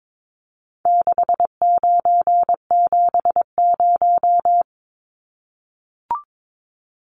Morse code, heard as a keyed tone, about 22 wpm, 700 Hz. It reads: 6970
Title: Numbers - 4-Digit 22wpm